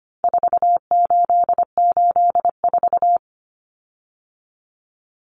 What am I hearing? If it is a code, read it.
Morse code, heard as a keyed tone, about 25 wpm, 700 Hz. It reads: 4884
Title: Common Cut Numbers - 4-Digit 25wpm